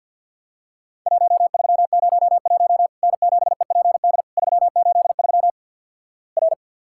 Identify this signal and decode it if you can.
Morse code, heard as a keyed tone, about 50 wpm, 700 Hz. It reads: 1301 N7EPD 383 R
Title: Worked All Europe DX Contest - QTC Exchange with Repeats - 50wpm